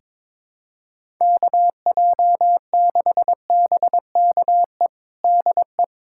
Morse code, heard as a keyed tone, about 22 wpm, 700 Hz. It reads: KJ6BKE DE
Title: USA Callsign DE Callsign - 6 Characters - 22wpm